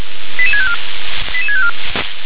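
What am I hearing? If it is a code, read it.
Five-tone selective call (ZVEI), heard as quick runs of five tones, tones of about 70 ms. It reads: 88654 88654
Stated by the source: ZVEI five-tone call received on an SDR.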